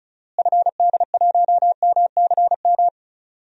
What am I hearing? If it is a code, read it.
Morse code, heard as a keyed tone, about 36 wpm, 700 Hz. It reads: FD1MCM
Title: DXCC-100 Call Signs - 6 Characters - Repeated 3x - 35wpm